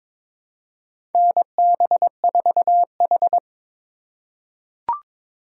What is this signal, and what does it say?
Morse code, heard as a keyed tone, about 23 wpm, 700 Hz. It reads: NB4H
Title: USA Call Signs - 4 Characters - 4x Character Spacing - 22wpm